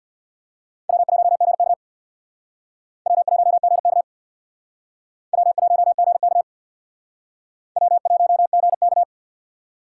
Morse code, heard as a keyed tone, about 64 wpm, 700 Hz. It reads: W1GK W1GK W1GK W1GK